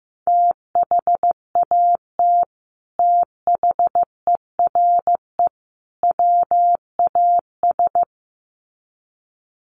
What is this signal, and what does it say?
Morse code, heard as a keyed tone, about 15 wpm, 700 Hz. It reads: THAT THERE WAS
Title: Sets of 3 Words 15wpm